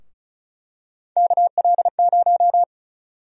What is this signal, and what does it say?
Morse code, heard as a keyed tone, about 34 wpm, 700 Hz. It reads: KL0
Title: USA Call Sign Prefixes - Repeated 2x - 35wpm